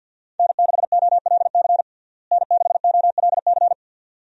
Morse code, heard as a keyed tone, about 53 wpm, 700 Hz. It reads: N6OLC N6OLC
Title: USA Call Signs - 5 Characters - Repeated 2x - 50wpm